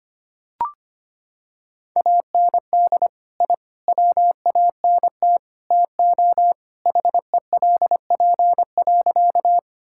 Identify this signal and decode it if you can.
Morse code, heard as a keyed tone, about 25 wpm, 700 Hz. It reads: AND I WANT TO HELP.
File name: practice / 25wpm / Sentences with 4-Word Phrases - 25wpm